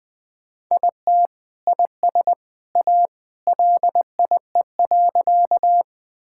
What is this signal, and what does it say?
Morse code, heard as a keyed tone, about 20 wpm, 700 Hz. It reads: IT IS A LIE.
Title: Sentences from Top 1600 Words 20wpm